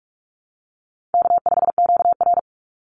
Morse code, heard as a keyed tone, about 44 wpm, 700 Hz. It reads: K5QR
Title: NA SOTA Chaser Callsigns 45wpm